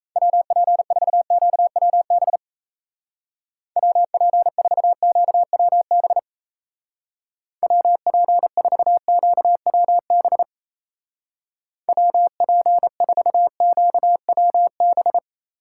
Morse code, heard as a keyed tone, about 36 wpm, 700 Hz. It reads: WP4QWB WP4QWB WP4QWB WP4QWB